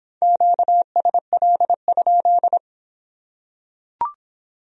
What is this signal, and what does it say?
Morse code, heard as a keyed tone, about 25 wpm, 700 Hz. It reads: QSL?